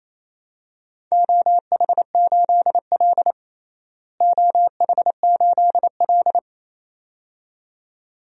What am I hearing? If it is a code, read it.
Morse code, heard as a keyed tone, about 28 wpm, 700 Hz. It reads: OH8L OH8L
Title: DXCC-100 Call Signs - 4 Characters - Repeated 2x - 28wpm